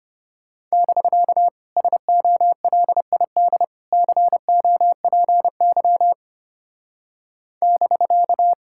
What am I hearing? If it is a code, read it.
Morse code, heard as a keyed tone, about 31 wpm, 700 Hz. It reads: <BK> SOLID COPY <BK>
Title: Speed-Racing Common QSO Phrases - 25wpm